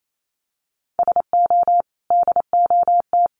Morse code, heard as a keyed tone, about 28 wpm, 700 Hz. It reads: SO DOT
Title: Sentences from Top 1500 Words 28wpm